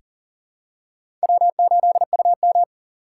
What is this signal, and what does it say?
Morse code, heard as a keyed tone, about 38 wpm, 690 Hz. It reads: W8UM
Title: USA Call Signs - 4 Characters - 2x Character Spacing - 40wpm